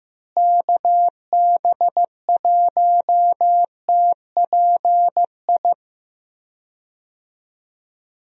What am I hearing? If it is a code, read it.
Morse code, heard as a keyed tone, about 15 wpm, 700 Hz. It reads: KB1TPI
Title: USA Call Signs - 6 Characters - Repeated 3x - 15wpm